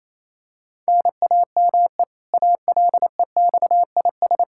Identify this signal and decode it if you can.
Morse code, heard as a keyed tone, about 28 wpm, 700 Hz. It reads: NAME ALEXIS
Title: Practice QSO - Exchange 1 - Beginning - 28wpm